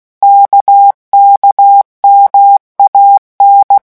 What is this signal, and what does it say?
Morse code, heard as a keyed tone, about 16 wpm, 800 Hz. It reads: KKMAN
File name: morse mp3 / revision3-vitesse 15 mots